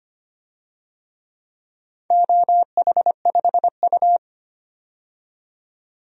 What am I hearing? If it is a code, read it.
Morse code, heard as a keyed tone, about 25 wpm, 700 Hz. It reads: OH5U
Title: DXCC-100 Call Signs - 4 Characters - 25wpm